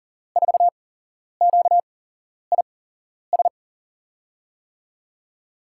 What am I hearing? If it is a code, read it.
Morse code, heard as a keyed tone, about 38 wpm, spaced as for 11 wpm, 700 Hz. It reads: 4QIS
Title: USA Call Signs - 6 Characters - 8x Character Spacing - 40wpm